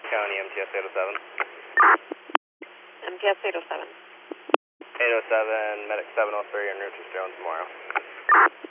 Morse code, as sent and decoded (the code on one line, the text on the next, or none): none